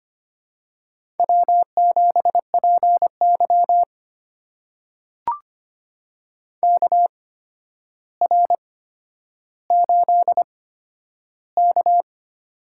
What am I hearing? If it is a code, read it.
Morse code, heard as a keyed tone, about 25 wpm, spaced as for 11 wpm, 700 Hz. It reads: W7PY KR8K
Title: USA Call Signs - 4 Characters - 8x Character Spacing - 25wpm